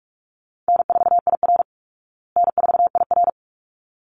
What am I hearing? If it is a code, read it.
Morse code, heard as a keyed tone, about 44 wpm, 700 Hz. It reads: N4IR N4IR
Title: Easy Call Signs Repeated 2x - 45wpm